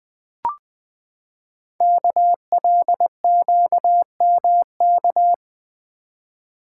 Morse code, heard as a keyed tone, about 20 wpm, 700 Hz. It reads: KLQMK
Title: Code Groups - 5 Letters - 20wpm